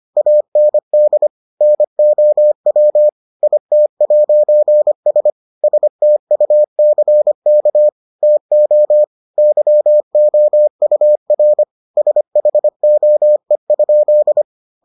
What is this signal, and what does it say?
Morse code, heard as a keyed tone, about 25 wpm, 600 Hz. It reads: AND NOW IT'S STUCK TO YOUR SHOE?